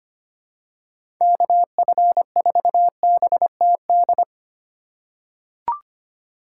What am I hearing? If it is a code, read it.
Morse code, heard as a keyed tone, about 25 wpm, 700 Hz. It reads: KF4BTD
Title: USA Call Signs - 3x Character Spacing - 25wpm